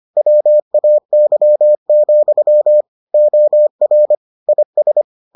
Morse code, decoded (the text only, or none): WAY, OR IS